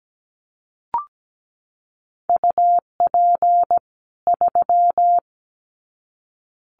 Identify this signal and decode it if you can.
Morse code, heard as a keyed tone, about 17 wpm, 700 Hz. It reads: UP 3
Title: Common QSO Phrases 17wpm